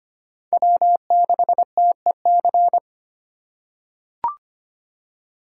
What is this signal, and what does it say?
Morse code, heard as a keyed tone, about 25 wpm, 700 Hz. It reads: W6TEC
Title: USA Call Signs - 5 Characters - 4x Character Spacing - 25wpm